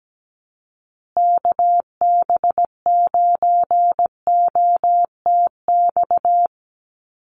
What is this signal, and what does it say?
Morse code, heard as a keyed tone, about 17 wpm, 700 Hz. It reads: KB9OTX
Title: USA Call Signs Repeated 2x 17wpm